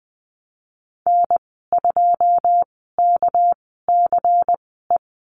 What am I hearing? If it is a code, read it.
Morse code, heard as a keyed tone, about 20 wpm, spaced as for 15 wpm, 700 Hz. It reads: N2KCE
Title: 207 Lesson 33g - Letter j - Call Signs 20wpm